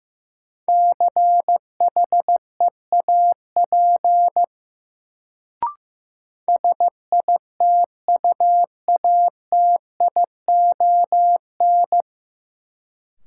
Morse code, heard as a keyed tone, about 15 wpm, 700 Hz. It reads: CHEAP SITUATION
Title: Sentences from Top 1200 Words - New Word Drill - 15wpm